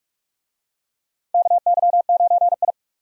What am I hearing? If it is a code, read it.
Morse code, heard as a keyed tone, about 43 wpm, 700 Hz. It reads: KY9I
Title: USA Call Signs - 4 Characters - Repeated 3x - 45wpm